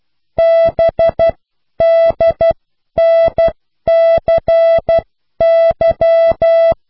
Morse code, heard as a keyed tone, about 12 wpm, spaced as for 10 wpm, 660 Hz. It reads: BDNCY